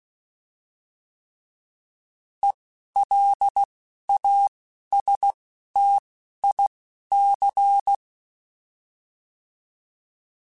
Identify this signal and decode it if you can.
Morse code, heard as a keyed tone, about 16 wpm, spaced as for 11 wpm, 800 Hz. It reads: ELASTIC